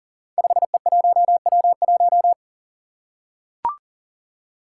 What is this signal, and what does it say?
Morse code, heard as a keyed tone, about 43 wpm, 700 Hz. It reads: 5E1W1
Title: Five Challenging Letter-Number Combinations 40wpm